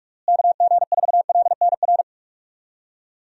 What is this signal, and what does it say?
Morse code, heard as a keyed tone, about 46 wpm, 700 Hz. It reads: KG4LNR
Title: USA Call Signs - 6 Characters - Repeated 3x - 45wpm